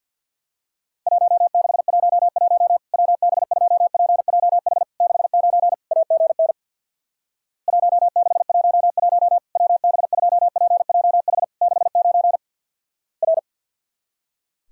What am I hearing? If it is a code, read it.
Morse code, heard as a keyed tone, about 50 wpm, 700 Hz. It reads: 1611 WB2PJH 69 AGN 1611 WB2PJH 69 R
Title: Worked All Europe DX Contest - QTC Exchange with Repeats - 50wpm